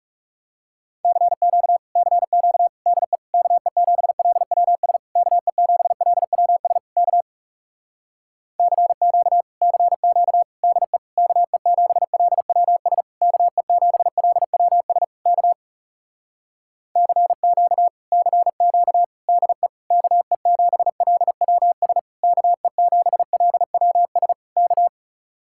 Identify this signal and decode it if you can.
Morse code, heard as a keyed tone, about 40 wpm, 700 Hz. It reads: CQ CQ DE KE7LWS KE7LWS K CQ CQ DE KE7LWS KE7LWS K CQ CQ DE KE7LWS KE7LWS K